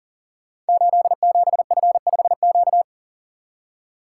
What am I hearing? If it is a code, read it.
Morse code, heard as a keyed tone, about 41 wpm, 700 Hz. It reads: 87F5Q
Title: Code Groups - 5 Letter:Numbers - 40wpm